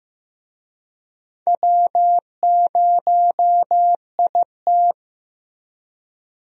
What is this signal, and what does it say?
Morse code, heard as a keyed tone, about 16 wpm, 700 Hz. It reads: W0IT
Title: USA Call Signs - 4 Characters - Repeated 2x - 15wpm